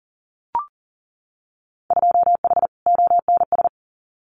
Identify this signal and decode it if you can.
Morse code, heard as a keyed tone, about 40 wpm, 700 Hz. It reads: 2H ONS
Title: Winter Field Day Exchanges Encore 40wpm